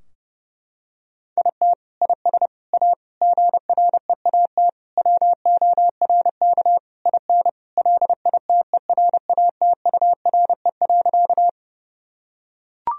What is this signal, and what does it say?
Morse code, heard as a keyed tone, about 32 wpm, 700 Hz. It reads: IT IS A GREAT WORK IN LITERATURE.
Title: Sentences from Top 1000 Words - Review Part 3 - 30wpm